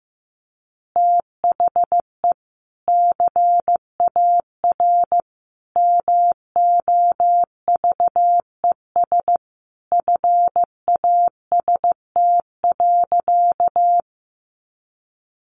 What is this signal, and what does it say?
Morse code, heard as a keyed tone, about 15 wpm, 700 Hz. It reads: THE CAR MOVES FAST.
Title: Easy Phrases 15wpm